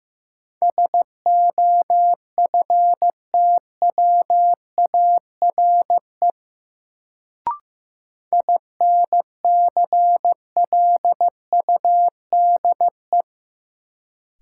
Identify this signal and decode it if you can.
Morse code, heard as a keyed tone, about 15 wpm, 700 Hz. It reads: SOFTWARE INCLUDE
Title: Sentences from Top 1000 Words - Review Part 3 - New Word Drill - 15wpm